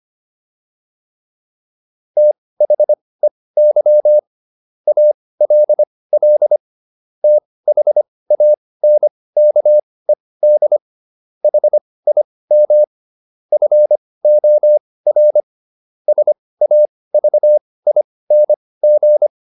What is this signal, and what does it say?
Morse code, heard as a keyed tone, about 25 wpm, spaced as for 18 wpm, 600 Hz. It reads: THEY ALL THANKED HIM FOR SAVING